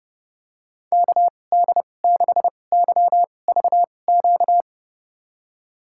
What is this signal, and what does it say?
Morse code, heard as a keyed tone, about 30 wpm, spaced as for 22 wpm, 700 Hz. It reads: KD6YVQ
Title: USA Call Signs - 6 Characters - 2x Character Spacing - 30wpm